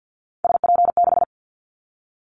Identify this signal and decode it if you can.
Morse code, heard as a keyed tone, about 50 wpm, 700 Hz. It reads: SP6
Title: Callsign Prefixes for Top 25 DXCC - 50wpm